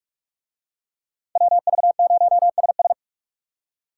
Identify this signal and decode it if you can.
Morse code, heard as a keyed tone, about 44 wpm, 700 Hz. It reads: WV0SS